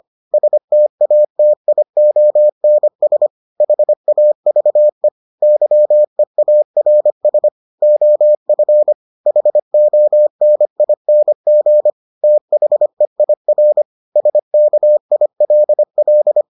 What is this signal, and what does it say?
Morse code, heard as a keyed tone, about 25 wpm, 600 Hz. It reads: STATIONS HAVE YEARS OF HONING THEIR SKILL